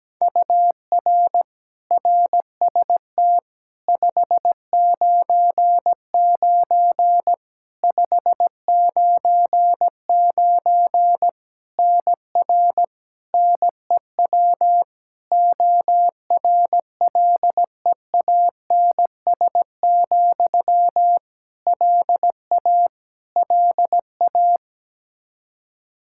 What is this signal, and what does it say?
Morse code, heard as a keyed tone, about 17 wpm, 700 Hz. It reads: UR RST 599 599 NR NEW ORLEANS, LA LA
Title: Practice QSO - Exchange 1 - Beginning - 17wpm